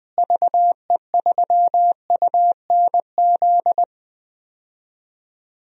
Morse code, heard as a keyed tone, about 20 wpm, 700 Hz. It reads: VE3UNZ
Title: DXCC-100 Call Signs - 6 Characters - Repeated 2x - 20wpm